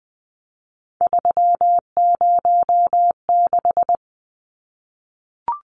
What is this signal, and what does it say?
Morse code, heard as a keyed tone, about 20 wpm, 700 Hz. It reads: 306
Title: Numbers - 3-Digit 20wpm